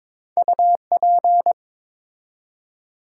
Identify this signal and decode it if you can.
Morse code, heard as a keyed tone, about 22 wpm, 700 Hz. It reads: UP
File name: Build A Phrase - 2-Word Phrases - 22wpm